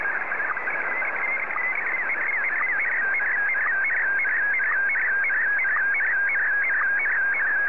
SSTV (slow-scan television), Scottie DX.